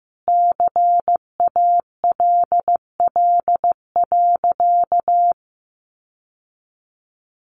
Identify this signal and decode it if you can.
Morse code, heard as a keyed tone, about 15 wpm, 700 Hz. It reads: CALL.
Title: Sentences from Top 900 Words 15wpm